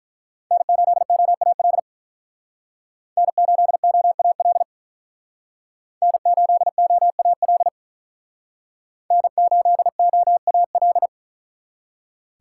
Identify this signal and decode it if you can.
Morse code, heard as a keyed tone, about 45 wpm, 700 Hz. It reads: N8OAL N8OAL N8OAL N8OAL